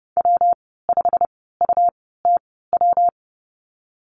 Morse code, heard as a keyed tone, about 31 wpm, spaced as for 17 wpm, 700 Hz. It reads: W5UTW
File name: Lesson 19g - Letter w - Call Signs 30wpm